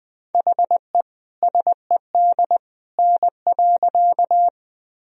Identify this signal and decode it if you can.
Morse code, heard as a keyed tone, about 20 wpm, 700 Hz. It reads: HE SED N.
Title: Sentences from Top 100 Words and Abbreviations 20wpm